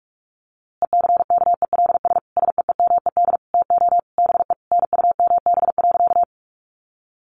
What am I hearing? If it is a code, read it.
Morse code, heard as a keyed tone, about 44 wpm, 700 Hz. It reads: ECKELS SEEMED TO BE NUMB.
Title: A Sound Of Thunder 45wpm